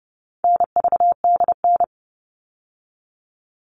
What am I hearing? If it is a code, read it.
Morse code, heard as a keyed tone, about 30 wpm, 700 Hz. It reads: NVDN
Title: All Cut Numbers - 4-Digit 30wpm